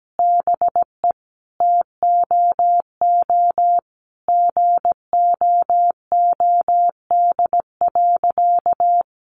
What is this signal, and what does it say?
Morse code, heard as a keyed tone, about 17 wpm, 700 Hz. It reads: BE TOO GOOD.
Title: Sentences from Top 1200 Words 17wpm